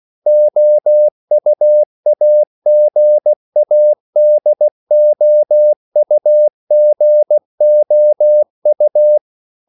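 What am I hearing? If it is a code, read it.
Morse code, heard as a keyed tone, about 16 wpm, 600 Hz. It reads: OUAGADOUGOU
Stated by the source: CW practice of national capitals at 16 words per minute, from ON6ZQ.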